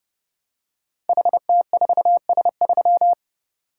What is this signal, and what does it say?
Morse code, heard as a keyed tone, about 30 wpm, 700 Hz. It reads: HT4S3
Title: Code Groups - 5 Letter:Numbers - 30wpm